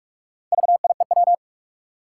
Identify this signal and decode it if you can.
Morse code, heard as a keyed tone, about 43 wpm, 700 Hz. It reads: VIEW
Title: Sentences from Top 900 Words - New Word Drill - 45wpm